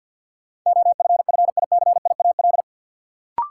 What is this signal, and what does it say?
Morse code, heard as a keyed tone, about 52 wpm, 700 Hz. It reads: OFFICIAL
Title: Sentences from Top 1000 Words - New Word Drill - 50wpm